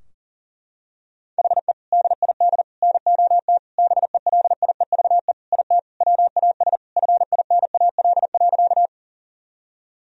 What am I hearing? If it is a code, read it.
Morse code, heard as a keyed tone, about 40 wpm, 700 Hz. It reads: HE DID NOT BELIEVE IT WAS FINAL.
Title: Sentences from Top 1000 Words - Review Part 2 - 40wpm